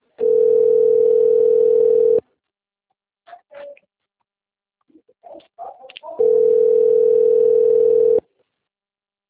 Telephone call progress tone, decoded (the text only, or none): ringback tone